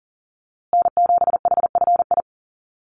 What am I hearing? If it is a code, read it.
Morse code, heard as a keyed tone, about 40 wpm, 700 Hz. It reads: N7HFI
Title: USA Call Signs Repeated 2x 40wpm